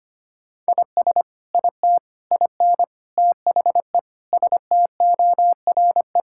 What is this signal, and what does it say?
Morse code, heard as a keyed tone, about 25 wpm, 700 Hz. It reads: IS IT IN THE STORE
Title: Sentences from Top 1000 Words - Review Part 5 - 25wpm